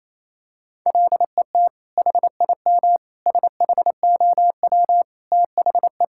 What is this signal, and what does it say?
Morse code, heard as a keyed tone, about 28 wpm, 700 Hz. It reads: LET HIM SHOW THE
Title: Sentences from Top 1600 Words 28wpm